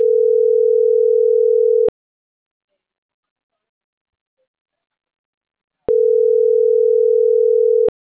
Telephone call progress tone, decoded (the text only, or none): ringback tone